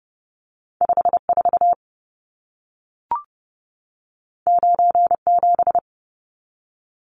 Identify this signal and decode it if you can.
Morse code, heard as a keyed tone, about 30 wpm, 700 Hz. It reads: 54 97
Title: Numbers - 2-Digit 30wpm